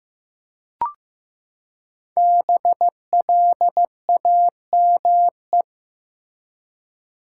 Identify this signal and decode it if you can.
Morse code, heard as a keyed tone, about 15 wpm, 700 Hz. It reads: BLAME
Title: Five Letter Words - 15wpm